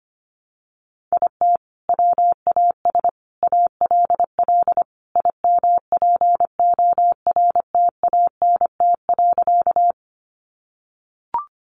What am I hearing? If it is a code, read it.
Morse code, heard as a keyed tone, about 25 wpm, 700 Hz. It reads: IT WAS ALL IMPORTANT.
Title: Sentences from Top 600 Words 25wpm